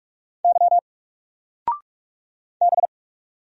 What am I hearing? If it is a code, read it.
Morse code, heard as a keyed tone, about 44 wpm, 700 Hz. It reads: Y B
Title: Single Character (Letter:Number:Punctuation:Prosign) 45wpm